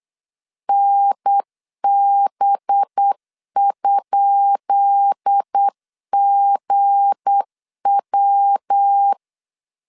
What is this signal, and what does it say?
Morse code, heard as a keyed tone, about 8 wpm, 800 Hz. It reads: NB?GW
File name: LEZIONI / 10 SIGNS / 10B INT 8wpm 3534